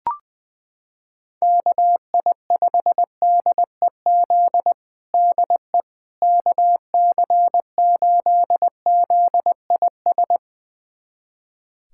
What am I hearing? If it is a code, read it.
Morse code, heard as a keyed tone, about 20 wpm, 700 Hz. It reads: KI5DEZ DE KC8ZIS
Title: USA Callsign DE Callsign 20wpm